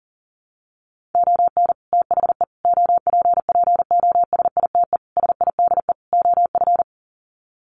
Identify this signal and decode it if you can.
Morse code, heard as a keyed tone, about 40 wpm, 700 Hz. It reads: ON THE OPPOSITE SIDE OF
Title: Sets of 5 Words Encore2 40wpm